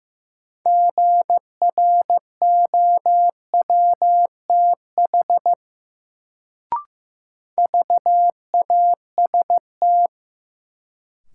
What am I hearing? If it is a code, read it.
Morse code, heard as a keyed tone, about 15 wpm, 700 Hz. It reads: GROWTH VAST
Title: Sentences from Top 1300 Words - New Word Drill Alt - 15wpm